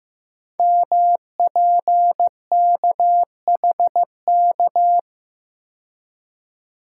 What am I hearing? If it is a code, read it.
Morse code, heard as a keyed tone, about 15 wpm, 700 Hz. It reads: MPKHK
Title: Code Groups - 5 Letter:Numbers - 15wpm